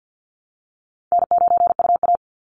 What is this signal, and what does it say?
Morse code, heard as a keyed tone, about 50 wpm, 700 Hz. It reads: N9UA
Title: Easy Call Signs - 50wpm